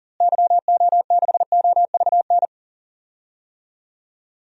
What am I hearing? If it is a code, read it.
Morse code, heard as a keyed tone, about 38 wpm, 700 Hz. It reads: YO6OVN
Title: DXCC-100 Call Signs - 6 Characters - Repeated 2x - 40wpm